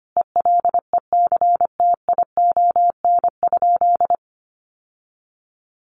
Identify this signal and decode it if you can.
Morse code, heard as a keyed tone, about 25 wpm, 700 Hz. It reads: ELECTION?
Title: Sentences from Top 1500 Words 25wpm